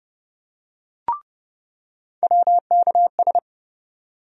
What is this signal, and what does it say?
Morse code, heard as a keyed tone, about 32 wpm, 700 Hz. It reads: WKS
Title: USA Call Sign Postfixes - 3 Letters 30wpm